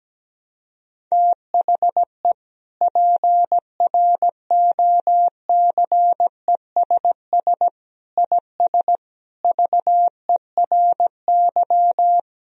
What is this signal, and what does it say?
Morse code, heard as a keyed tone, about 17 wpm, 700 Hz. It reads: THE PROCESS IS VERY
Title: Sentences from Top 1000 Words - Review Part 3 - 17wpm